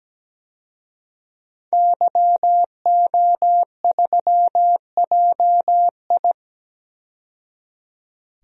Morse code, heard as a keyed tone, about 17 wpm, 700 Hz. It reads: YO3JI
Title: DXCC-100 Call Signs - 5 Characters - 17wpm